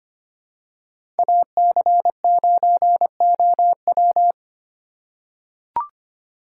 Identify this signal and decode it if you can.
Morse code, heard as a keyed tone, about 25 wpm, 700 Hz. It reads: AC9OW
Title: USA Call Signs - 5 Characters - 4x Character Spacing - 25wpm